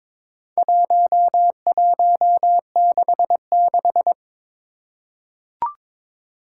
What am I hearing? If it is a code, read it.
Morse code, heard as a keyed tone, about 22 wpm, 700 Hz. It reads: 1166
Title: Four Challenging Letter-Number Combinations 22wpm